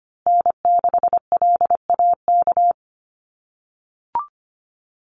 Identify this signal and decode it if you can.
Morse code, heard as a keyed tone, about 25 wpm, 700 Hz. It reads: N6LAK
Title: USA Call Signs Repeated 2x 25wpm